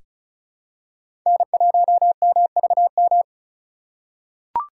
Morse code, heard as a keyed tone, about 34 wpm, 700 Hz. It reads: N1MVM
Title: USA Call Signs - 5 Characters - 4x Character Spacing - 35wpm